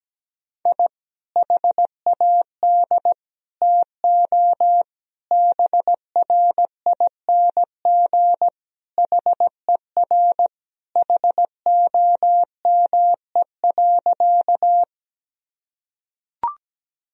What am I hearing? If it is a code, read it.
Morse code, heard as a keyed tone, about 17 wpm, 700 Hz. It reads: I HAD TO BRING HER HOME.
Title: Sentences from Top 1000 Words - Review Part 3 - 17wpm